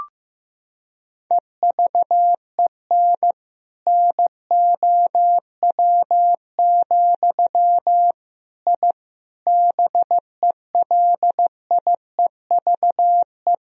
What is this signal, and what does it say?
Morse code, heard as a keyed tone, about 15 wpm, 700 Hz. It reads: EVEN NOW, I BELIEVE
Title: Sentences from Top 1000 Words - Review Part 1 - 15wpm